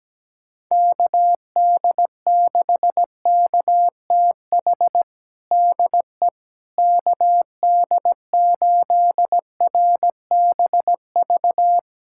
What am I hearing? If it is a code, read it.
Morse code, heard as a keyed tone, about 17 wpm, 700 Hz. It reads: KD6KTH DE KD8RBV
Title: USA Callsign DE Callsign - 6 Characters - 17wpm